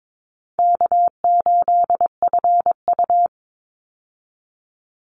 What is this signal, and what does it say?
Morse code, heard as a keyed tone, about 22 wpm, 700 Hz. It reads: K8FU
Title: Easy Call Signs Repeated 3x - 22wpm